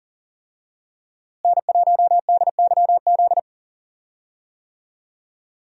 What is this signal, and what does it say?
Morse code, heard as a keyed tone, about 37 wpm, 700 Hz. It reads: N1DYZ
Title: USA Call Signs - 5 Characters - Repeated 3x - 40wpm